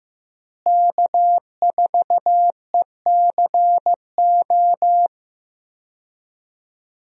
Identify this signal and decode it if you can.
Morse code, heard as a keyed tone, about 16 wpm, 700 Hz. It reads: K4ECO
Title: USA Call Signs - 5 Characters - Repeated 3x - 15wpm